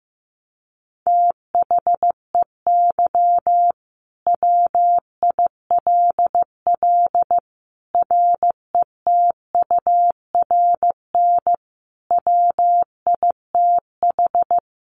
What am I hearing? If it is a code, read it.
Morse code, heard as a keyed tone, about 15 wpm, 700 Hz. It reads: THEY WILL RETURN WITH
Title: Sentences from Top 800 Words 15wpm